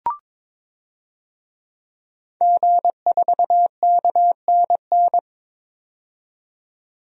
Morse code, heard as a keyed tone, about 22 wpm, 700 Hz. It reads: G4KNN
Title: DXCC-100 Call Signs - 5 Characters - 22wpm